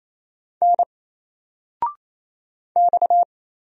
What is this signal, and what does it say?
Morse code, heard as a keyed tone, about 28 wpm, 700 Hz. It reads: N X
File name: Single-Letter-Number-28wpm